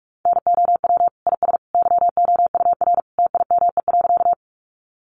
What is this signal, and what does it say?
Morse code, heard as a keyed tone, about 44 wpm, 700 Hz. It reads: NOW IS YOUR TIME.
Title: Sentences from Top 100 Words 45wpm